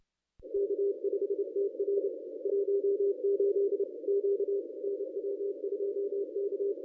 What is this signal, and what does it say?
Morse code, heard as a keyed tone, about 29 wpm, 400 Hz. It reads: K5TR 18 Q NW2K
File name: contests / ss-cw-k5tr-op-n5rz -1000